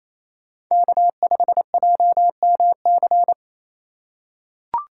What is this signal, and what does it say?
Morse code, heard as a keyed tone, about 28 wpm, 700 Hz. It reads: K5JMC
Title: USA Call Signs - 5 Characters - 4x Character Spacing - 28wpm